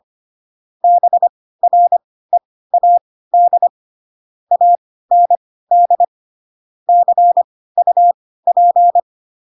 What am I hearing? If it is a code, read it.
Morse code, heard as a keyed tone, about 25 wpm, spaced as for 16 wpm, 700 Hz. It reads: BREAD AND CUP
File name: Pinocchio / Pinocchio ch2 10 25 ik1hge